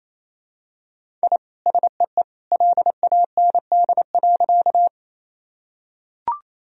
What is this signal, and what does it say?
Morse code, heard as a keyed tone, about 28 wpm, 700 Hz. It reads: I SEE LAND.
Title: Sentences from Top 1000 Words - Review Part 1 - 28wpm